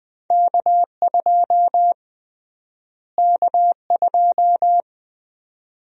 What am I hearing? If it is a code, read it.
Morse code, heard as a keyed tone, about 20 wpm, 700 Hz. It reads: K2 K2
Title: USA Call Sign Prefixes - Repeated 2x - 20wpm